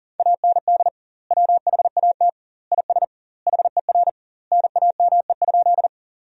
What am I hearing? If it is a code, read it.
Morse code, heard as a keyed tone, about 40 wpm, 700 Hz. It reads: AND WHAT IS HER NAME?
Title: 2x Word-Spacing - Sentences from Top 200 Words - 40wpm